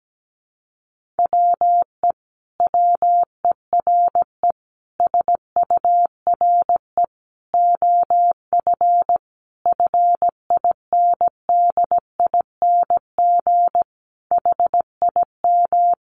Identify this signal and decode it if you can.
Morse code, heard as a keyed tone, about 17 wpm, 700 Hz. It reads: WE WERE SURE OF FINDING HIM
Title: Sentences from Top 1100 Words 17wpm